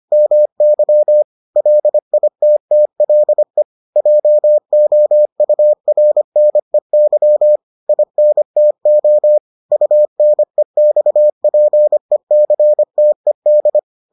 Morse code, heard as a keyed tone, about 25 wpm, 600 Hz. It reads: MY LITTLE JOURNEY INTO UNEXPECTED